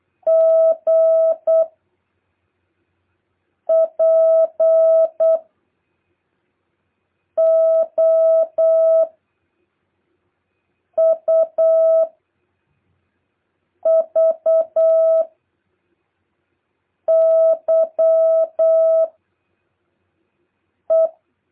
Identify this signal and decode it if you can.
Morse code, heard as a keyed tone, about 8 wpm, 650 Hz. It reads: G P O U V Y E
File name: jh8cbh / 2023cw